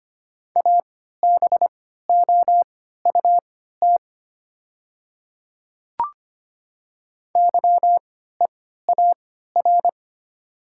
Morse code, heard as a keyed tone, about 25 wpm, spaced as for 14 wpm, 700 Hz. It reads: ABOUT YEAR
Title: 3x Character Spacing - Top 200 Words - 25wpm